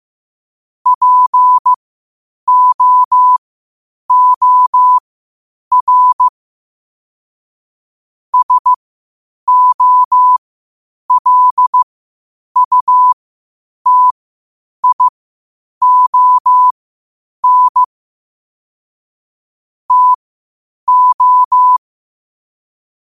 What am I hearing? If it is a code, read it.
Morse code, heard as a keyed tone, about 15 wpm, spaced as for 8 wpm, 1000 Hz. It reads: POOR SOLUTION TO
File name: qotdfiles / 8wpm